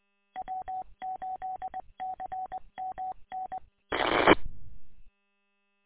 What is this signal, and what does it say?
Morse code, heard as a keyed tone, about 21 wpm, 750 Hz. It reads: W8CMN S
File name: w8cmn- -combined